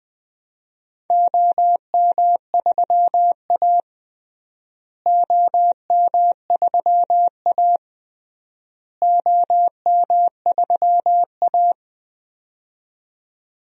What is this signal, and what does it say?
Morse code, heard as a keyed tone, about 21 wpm, 700 Hz. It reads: OM3A OM3A OM3A
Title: DXCC-100 Call Signs - 4 Characters - Repeated 3x - 20wpm